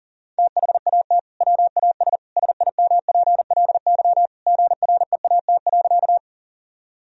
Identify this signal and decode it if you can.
Morse code, heard as a keyed tone, about 37 wpm, 700 Hz. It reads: THAT WAS SIMPLY GREAT.
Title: Sentences from Top 1000 Words - Review Part 4 - 40wpm